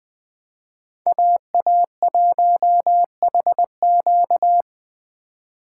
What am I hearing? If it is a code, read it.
Morse code, heard as a keyed tone, about 20 wpm, 700 Hz. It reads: AA1HQ
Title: USA Call Signs - 5 Characters - 4x Character Spacing - 20wpm